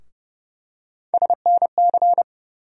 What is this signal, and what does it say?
Morse code, heard as a keyed tone, about 32 wpm, 700 Hz. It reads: SNC
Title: USA Call Sign Postfixes - 3 Letters 30wpm